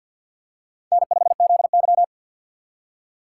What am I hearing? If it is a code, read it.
Morse code, heard as a keyed tone, about 53 wpm, 700 Hz. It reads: N5ZY
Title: USA Call Signs - 3x Character Spacing - 50wpm